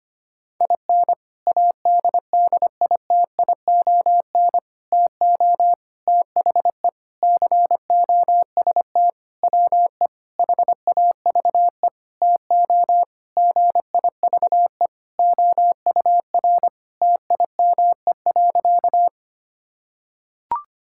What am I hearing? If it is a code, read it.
Morse code, heard as a keyed tone, about 25 wpm, 700 Hz. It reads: IN ADDITION TO THE COST WE HAVE TO GIVE OUR TIME.
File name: practice / 25wpm / Sentences with 4-Word Phrases - 25wpm